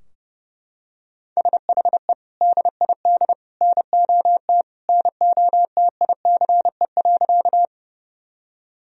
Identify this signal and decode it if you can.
Morse code, heard as a keyed tone, about 32 wpm, 700 Hz. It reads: SHE DID NOT NOTICE.
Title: Sentences from Top 1000 Words - Review Part 2 - 30wpm